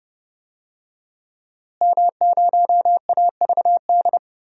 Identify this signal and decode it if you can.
Morse code, heard as a keyed tone, about 31 wpm, 700 Hz. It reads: M0AVD
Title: DXCC-100 Call Signs - 5 Characters - Repeated 2x - 30wpm